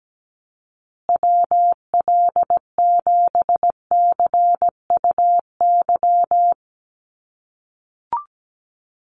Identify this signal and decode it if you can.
Morse code, heard as a keyed tone, about 17 wpm, 700 Hz. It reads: WL7CUY
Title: USA Call Signs Repeated 2x 17wpm